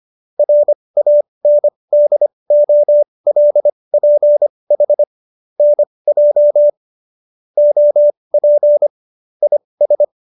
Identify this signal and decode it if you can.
Morse code, heard as a keyed tone, about 25 wpm, spaced as for 20 wpm, 590 Hz. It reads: RANDOLPH NJ OP IS